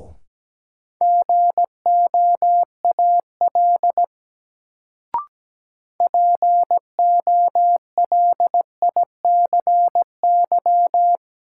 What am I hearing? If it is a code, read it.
Morse code, heard as a keyed tone, about 17 wpm, 700 Hz. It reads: GOAL POLICY
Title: Sentences from Top 1000 Words - Review Part 5 - New Word Drill - 17wpm